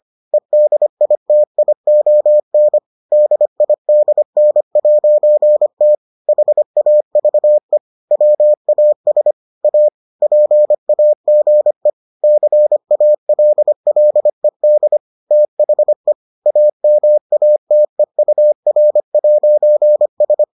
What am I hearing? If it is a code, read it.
Morse code, heard as a keyed tone, about 25 wpm, 600 Hz. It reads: EDITION DIDN'T HAVE WAS A PAGE CALLED THE AMATEUR'S